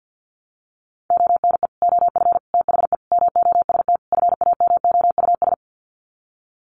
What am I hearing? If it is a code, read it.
Morse code, heard as a keyed tone, about 50 wpm, 700 Hz. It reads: ONE OF THE MOST FAMOUS
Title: Sets of 5 Words Encore2 50wpm